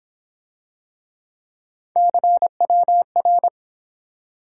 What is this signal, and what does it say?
Morse code, heard as a keyed tone, about 29 wpm, 700 Hz. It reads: CWR